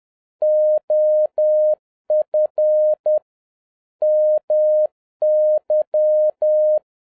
Morse code, heard as a keyed tone, about 10 wpm, 610 Hz. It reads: OF MY
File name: 2011-07-mp3 / n41 EXCITING QRP CW OPERATION 10wpm